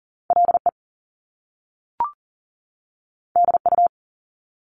Morse code, heard as a keyed tone, about 40 wpm, 700 Hz. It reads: LE DU
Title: Top Two Letter Combinations in Words 40wpm